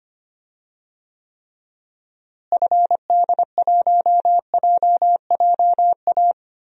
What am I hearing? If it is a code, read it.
Morse code, heard as a keyed tone, about 25 wpm, 700 Hz. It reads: FD1JJA
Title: DXCC-100 Call Signs - 6 Characters - Repeated 3x - 25wpm